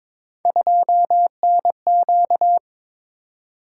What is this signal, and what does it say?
Morse code, heard as a keyed tone, about 22 wpm, 700 Hz. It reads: 2NQ
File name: practice / 22wpm / Three Letter-Number Combinations - 22wpm